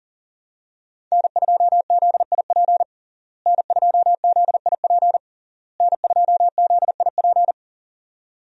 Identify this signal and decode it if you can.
Morse code, heard as a keyed tone, about 40 wpm, 690 Hz. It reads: N2ZIP N2ZIP N2ZIP
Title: USA Call Signs - 5 Characters - Repeated 3x - 40wpm